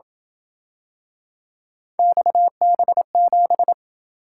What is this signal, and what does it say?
Morse code, heard as a keyed tone, about 27 wpm, 700 Hz. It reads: XB7